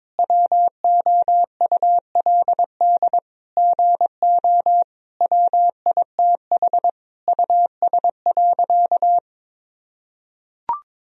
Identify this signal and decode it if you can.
Morse code, heard as a keyed tone, about 22 wpm, 700 Hz. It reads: WOULD GO WITH US.
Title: Sentences from Top 1000 Words - Review Part 3 - 22wpm